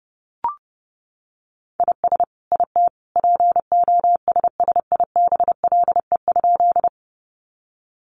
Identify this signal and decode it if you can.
Morse code, heard as a keyed tone, about 30 wpm, 700 Hz. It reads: IS IT POSSIBLE?
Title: Sentences from Top 500 Words 30wpm